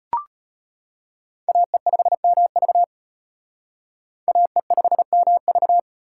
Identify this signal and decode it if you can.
Morse code, heard as a keyed tone, about 32 wpm, 700 Hz. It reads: AE5MV AE5MV